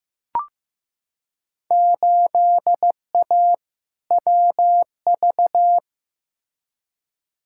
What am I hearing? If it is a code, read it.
Morse code, heard as a keyed tone, about 15 wpm, 700 Hz. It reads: 8A WV
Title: Field Day Exchanges Encore - 15wpm